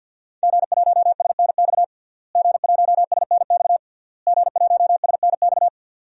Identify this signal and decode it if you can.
Morse code, heard as a keyed tone, about 52 wpm, 700 Hz. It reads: G1SNX G1SNX G1SNX
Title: DXCC-100 Call Signs - 5 Characters - Repeated 3x - 50wpm